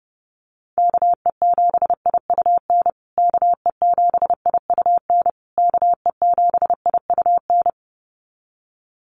Morse code, heard as a keyed tone, about 30 wpm, 700 Hz. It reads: KE7IUN KE7IUN KE7IUN
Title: USA Call Signs Repeated 3x 30wpm